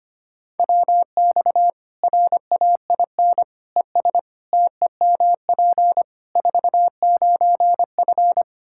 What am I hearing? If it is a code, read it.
Morse code, heard as a keyed tone, about 25 wpm, 700 Hz. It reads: WX RAIN ES TEMP 49F
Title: Practice QSO - Exchange 2 - Info - 25wpm